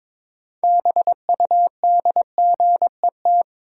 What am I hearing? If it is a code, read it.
Morse code, heard as a keyed tone, about 22 wpm, 700 Hz. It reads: BUDGET
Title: Sentences from Top 1200 Words - New Word Drill - 22wpm